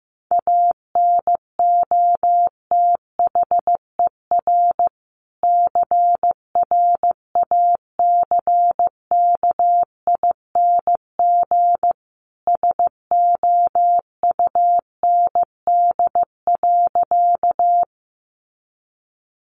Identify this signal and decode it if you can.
Morse code, heard as a keyed tone, about 15 wpm, 700 Hz. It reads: ANOTHER CRACKING SOUND.
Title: A Sound Of Thunder 15wpm